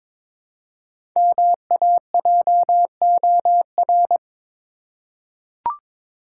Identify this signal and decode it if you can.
Morse code, heard as a keyed tone, about 22 wpm, 700 Hz. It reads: MAJOR
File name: 5 Letter Words - First 3 Letters Given - 22wpm